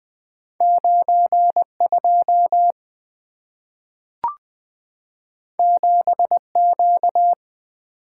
Morse code, heard as a keyed tone, about 20 wpm, 700 Hz. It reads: 92 7Q
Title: Two Letter-Number Combinations - 20wpm